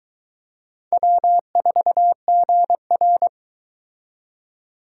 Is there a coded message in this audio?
Yes: Morse code from 0.9 to 3.3 s, heard as a keyed tone, about 21 wpm, 700 Hz. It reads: W4GR